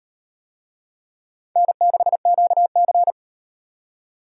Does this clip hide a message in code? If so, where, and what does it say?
Yes: Morse code from 1.6 to 3.1 s, heard as a keyed tone, about 36 wpm, 700 Hz. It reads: N6QC